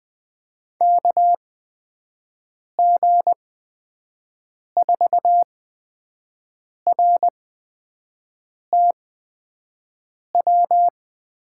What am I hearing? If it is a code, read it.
Morse code, heard as a keyed tone, about 20 wpm, spaced as for 5 wpm, 700 Hz. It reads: KG4RTW
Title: USA Call Signs - 6 Characters - 8x Character Spacing - 20wpm